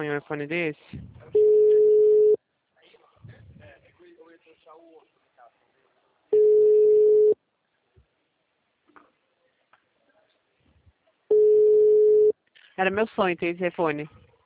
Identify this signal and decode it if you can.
Ringback tone (Europe, 425 Hz), with speech.
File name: out- -2005- - - .54797